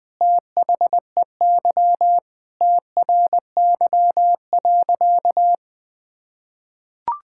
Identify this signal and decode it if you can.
Morse code, heard as a keyed tone, about 20 wpm, 700 Hz. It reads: THEY TRY.
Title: Sentences with 2-Word Phrases - 20wpm